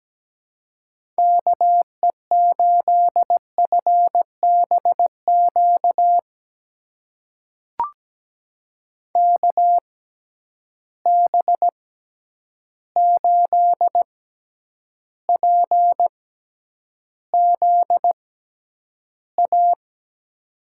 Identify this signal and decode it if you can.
Morse code, heard as a keyed tone, about 17 wpm, spaced as for 9 wpm, 700 Hz. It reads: KE8FBQ KB8PZA
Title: USA Call Signs - 6 Characters - 6x Character Spacing - 17wpm